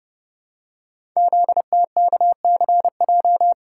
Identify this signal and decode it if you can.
Morse code, heard as a keyed tone, about 30 wpm, 700 Hz. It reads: ZTKCJ
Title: Code Groups - 5 Letters - 30wpm